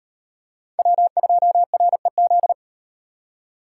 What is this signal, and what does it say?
Morse code, heard as a keyed tone, about 38 wpm, 700 Hz. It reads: W2REZ